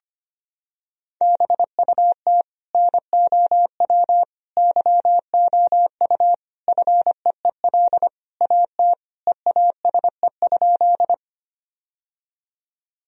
Morse code, heard as a keyed tone, about 25 wpm, 700 Hz. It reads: BUT NOW YOU FEEL AT EASE?
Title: Sentences from Top 1000 Words - Review Part 1 - 25wpm